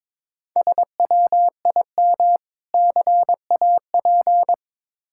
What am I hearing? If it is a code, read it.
Morse code, heard as a keyed tone, about 22 wpm, 700 Hz. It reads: SWIM CAP
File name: Difficult Words with Hints - 22wpm